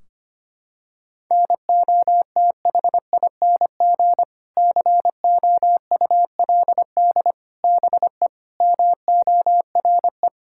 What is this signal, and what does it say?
Morse code, heard as a keyed tone, about 25 wpm, 700 Hz. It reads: NOTHING COULD BE MORE
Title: Sentences from Top 1600 Words 25wpm